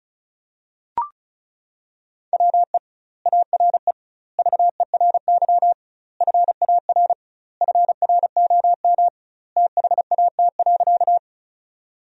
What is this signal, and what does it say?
Morse code, heard as a keyed tone, about 36 wpm, 700 Hz. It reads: WE ARE VERY FAR FROM THAT.
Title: 2x Word-Spacing - Sentences from Top 300 Words - 35wpm